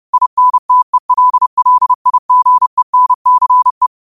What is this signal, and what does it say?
Morse code, heard as a keyed tone, about 30 wpm, 1000 Hz. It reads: INTELLIGENCE
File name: qotdfiles / 30wpm